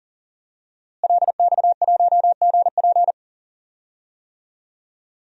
Morse code, heard as a keyed tone, about 40 wpm, 700 Hz. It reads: LX1GP
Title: DXCC-100 Call Signs - 5 Characters - Repeated 2x - 40wpm